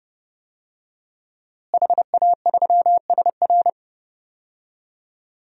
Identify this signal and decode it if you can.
Morse code, heard as a keyed tone, about 30 wpm, 700 Hz. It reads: HA3SR
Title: DXCC-100 Call Signs - 5 Characters - Repeated 3x - 30wpm